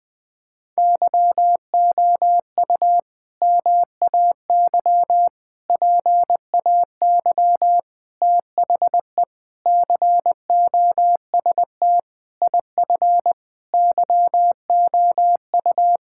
Morse code, heard as a keyed tone, about 21 wpm, 700 Hz. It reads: YOU MAY PAY THE COST IF YOU
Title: Sets of 8 Words - 20wpm